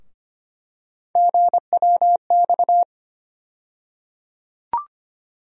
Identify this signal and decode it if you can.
Morse code, heard as a keyed tone, about 25 wpm, 700 Hz. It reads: GWX
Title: USA Call Sign Postfixes - 3 Letters 25wpm